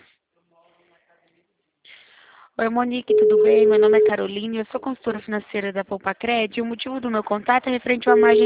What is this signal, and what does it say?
Ringback tone (Europe, 425 Hz), with speech.
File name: out- -2011- - - .67754